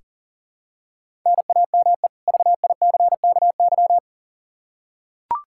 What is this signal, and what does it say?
Morse code, heard as a keyed tone, about 40 wpm, 700 Hz. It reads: NAME VICKY
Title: Practice QSO - Exchange 1 - Beginning - 40wpm